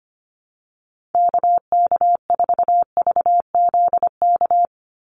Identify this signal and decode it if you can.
Morse code, heard as a keyed tone, about 25 wpm, 700 Hz. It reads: KK4VZK
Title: USA Call Signs 25wpm